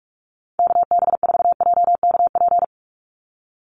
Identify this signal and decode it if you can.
Morse code, heard as a keyed tone, about 44 wpm, 700 Hz. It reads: KB4JKP
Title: USA Call Signs 45wpm